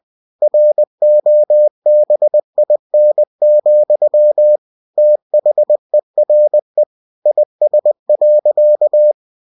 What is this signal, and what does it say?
Morse code, heard as a keyed tone, about 20 wpm, 600 Hz. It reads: ROBIN, THERE IS.
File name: cw-books / winnie-the-pooh / winnie-20wpm-page-0096